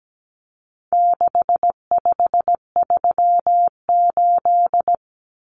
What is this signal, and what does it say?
Morse code, heard as a keyed tone, about 17 wpm, 700 Hz. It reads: 6538
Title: Common Cut Numbers - 4-Digit 17wpm